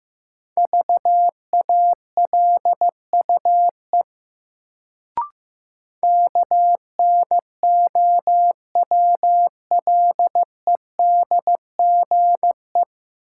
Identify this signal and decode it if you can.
Morse code, heard as a keyed tone, about 15 wpm, 700 Hz. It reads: VALUE KNOWLEDGE
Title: Sentences from Top 600 Words - New Word Drill - 15wpm